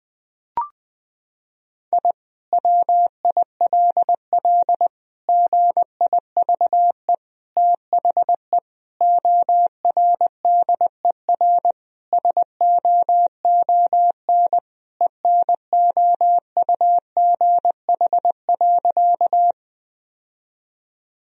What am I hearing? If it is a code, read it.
Morse code, heard as a keyed tone, about 20 wpm, 700 Hz. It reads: I WILL GIVE THE ORDER SOON ENOUGH.
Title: Sentences from Top 1000 Words - Review Part 2 - 20wpm